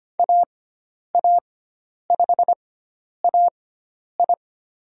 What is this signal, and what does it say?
Morse code, heard as a keyed tone, about 25 wpm, spaced as for 10 wpm, 700 Hz. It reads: AA5AI
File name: Lesson 16g - Letter c - Call Signs 25wpm